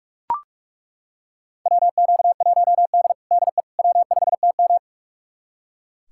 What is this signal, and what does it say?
Morse code, heard as a keyed tone, about 42 wpm, 700 Hz. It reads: WQ1D DE W5TM
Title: USA Callsign DE Callsign - 4 Characters - 45wpm